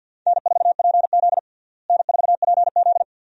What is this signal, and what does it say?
Morse code, heard as a keyed tone, about 49 wpm, 700 Hz. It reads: N4PZ N4PZ
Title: USA Call Signs - 4 Characters - Repeated 2x - 50wpm